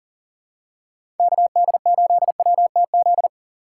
Morse code, heard as a keyed tone, about 39 wpm, 700 Hz. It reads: KD8WTZ
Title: USA Call Signs - 2x Character Spacing - 40wpm